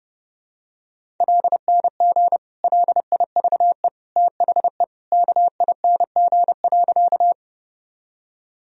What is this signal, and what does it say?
Morse code, heard as a keyed tone, about 30 wpm, 700 Hz. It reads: LNG LIVE THE KING.
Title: Sentences from Top 400 Words and Abbreviations 30wpm